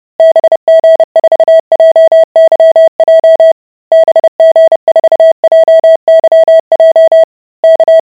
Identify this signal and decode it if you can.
Morse code, heard as a keyed tone, about 30 wpm, 650 Hz. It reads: BG4JYJ BG4JYJ K